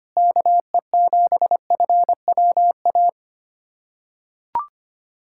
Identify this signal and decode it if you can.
Morse code, heard as a keyed tone, about 25 wpm, 700 Hz. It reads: KE7FWA
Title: USA Callsign DE Callsign 25wpm